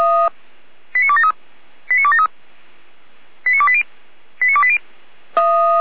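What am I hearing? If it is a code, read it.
Five-tone selective call (ZVEI), heard as quick runs of five tones, tones of about 70 ms. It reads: 89273 89273 89280 89280
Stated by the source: ZVEI five-tone selective call received by SDR, logged by feuerwehr-orlamuende.org, a fire brigade.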